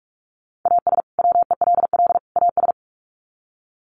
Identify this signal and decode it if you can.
Morse code, heard as a keyed tone, about 44 wpm, 700 Hz. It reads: AS WELL AS
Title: Sets of 3 Words 45wpm